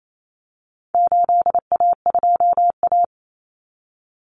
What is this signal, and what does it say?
Morse code, heard as a keyed tone, about 28 wpm, 700 Hz. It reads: 8A2A
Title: Common Cut Numbers - 4-Digit 28wpm